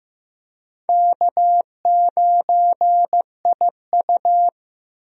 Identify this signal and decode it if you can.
Morse code, heard as a keyed tone, about 15 wpm, 700 Hz. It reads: K9IU
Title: USA Call Signs - 4 Characters - 15wpm